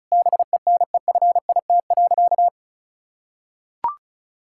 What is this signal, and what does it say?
Morse code, heard as a keyed tone, about 36 wpm, 700 Hz. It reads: BENEFIT.
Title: Sentences from Top 1000 Words - Review Part 6 - 35wpm